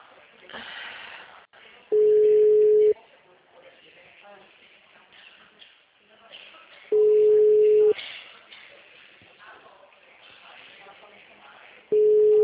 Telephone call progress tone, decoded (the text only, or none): ringback tone